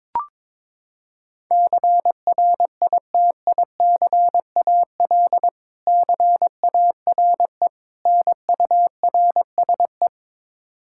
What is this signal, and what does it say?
Morse code, heard as a keyed tone, about 21 wpm, 700 Hz. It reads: CRITICAL CARE NURSE
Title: Practice QSO - Exchange 3 - More Info - 22wpm